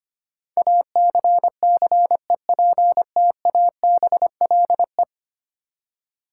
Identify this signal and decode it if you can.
Morse code, heard as a keyed tone, about 25 wpm, 690 Hz. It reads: ACCEPTABLE
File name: 10 Letter Words - First 3 Letters Given - 25wpm